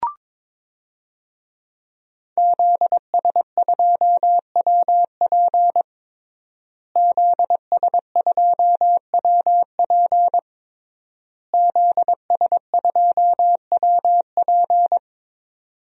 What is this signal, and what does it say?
Morse code, heard as a keyed tone, about 22 wpm, 700 Hz. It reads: ZS2WP ZS2WP ZS2WP
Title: DXCC-100 Call Signs - 5 Characters - Repeated 3x - 22wpm